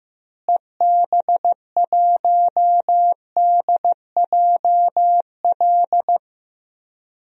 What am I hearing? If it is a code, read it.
Morse code, heard as a keyed tone, about 15 wpm, 700 Hz. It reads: EB1DJL
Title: DXCC-100 Call Signs - 6 Characters - Repeated 2x - 15wpm